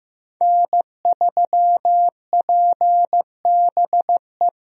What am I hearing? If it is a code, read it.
Morse code, heard as a keyed tone, about 15 wpm, 700 Hz. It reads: N3PBE
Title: USA Call Signs - 5 Characters - Repeated 3x - 15wpm